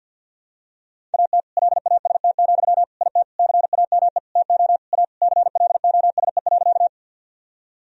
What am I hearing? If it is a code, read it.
Morse code, heard as a keyed tone, about 52 wpm, 700 Hz. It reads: AT LAST, IT CAME TO A CLOSE.
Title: Sentences from Top 1000 Words - Review Part 1 - 50wpm